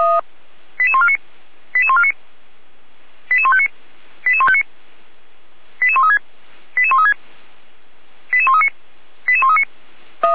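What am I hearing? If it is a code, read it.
Five-tone selective call (ZVEI), heard as quick runs of five tones, tones of about 70 ms. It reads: 88149 88149 88159 88159 88136 88136 88139 88139
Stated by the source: ZVEI five-tone selective call 88335 received by SDR.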